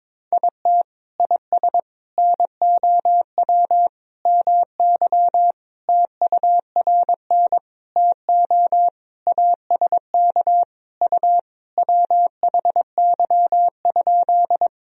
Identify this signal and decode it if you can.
Morse code, heard as a keyed tone, about 22 wpm, 700 Hz. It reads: IT IS NOW MY TURN TO ASK U WHY?
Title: Sentences from Top 200 Words and Abbreviations 22wpm